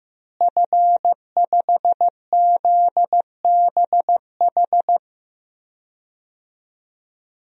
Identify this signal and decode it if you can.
Morse code, heard as a keyed tone, about 15 wpm, 700 Hz. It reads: F5ZBH
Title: DXCC-100 Call Signs - 5 Characters - Repeated 2x - 15wpm